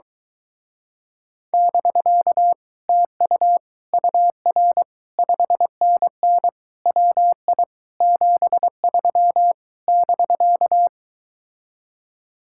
Morse code, heard as a keyed tone, about 23 wpm, 700 Hz. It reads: <BK> TU UR 5NN WI 73 <BK>